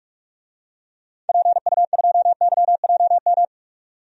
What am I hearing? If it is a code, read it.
Morse code, heard as a keyed tone, about 47 wpm, 700 Hz. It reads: PU2YJM
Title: DXCC-100 Call Signs - 6 Characters - 45wpm